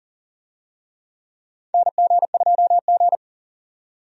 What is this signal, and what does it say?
Morse code, heard as a keyed tone, about 40 wpm, 700 Hz. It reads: NG2G
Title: USA Call Signs - 4 Characters - 40wpm